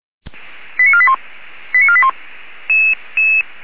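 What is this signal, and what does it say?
Five-tone selective call (ZVEI), heard as quick runs of five tones, tones of about 80 ms. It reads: 89481 89481
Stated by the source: ZVEI five-tone selective call 85371 received by SDR.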